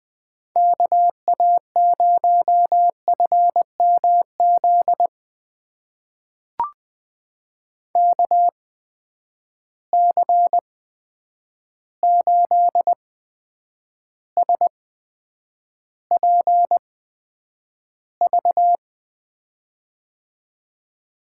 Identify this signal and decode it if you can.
Morse code, heard as a keyed tone, about 20 wpm, spaced as for 9 wpm, 700 Hz. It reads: KA0FMZ KC8SPV
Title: USA Call Signs - 6 Characters - 8x Character Spacing - 20wpm